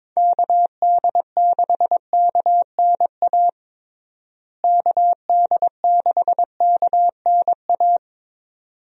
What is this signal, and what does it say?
Morse code, heard as a keyed tone, about 22 wpm, 700 Hz. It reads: KD6KNA KD6KNA
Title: USA Call Signs - 6 Characters - Repeated 2x - 22wpm